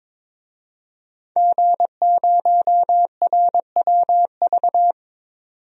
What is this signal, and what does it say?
Morse code, heard as a keyed tone, about 22 wpm, 700 Hz. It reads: G0RWV
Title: DXCC-100 Call Signs - 5 Characters - 22wpm